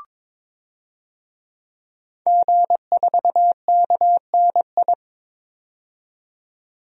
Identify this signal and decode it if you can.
Morse code, heard as a keyed tone, about 22 wpm, 700 Hz. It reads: G4KNI
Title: DXCC-100 Call Signs - 5 Characters - 22wpm